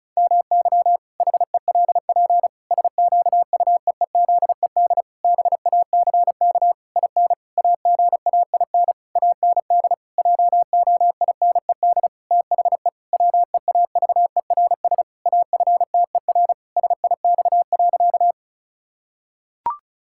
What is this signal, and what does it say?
Morse code, heard as a keyed tone, about 34 wpm, 700 Hz. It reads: MY HELP SQUEEZED BACK IN AGAIN AND JOINED THE WEAVERS AFTER SIX.
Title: Pangram Challenge v2 - 35wpm